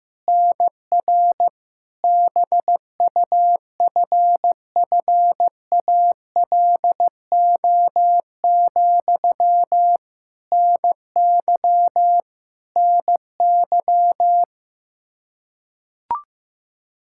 Morse code, heard as a keyed tone, about 15 wpm, 700 Hz. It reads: NR BUFFALO, NY NY
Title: K1USN SST QSO Practice 15wpm